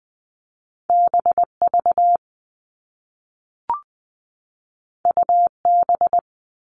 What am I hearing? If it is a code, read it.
Morse code, heard as a keyed tone, about 20 wpm, 700 Hz. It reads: BV UB
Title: All Cut Numbers - 2-Digit 20wpm